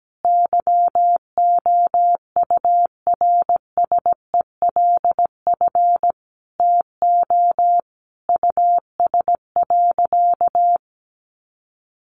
Morse code, heard as a keyed tone, about 17 wpm, 700 Hz. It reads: YOURSELF TO US.
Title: Sentences from Top 600 Words 17wpm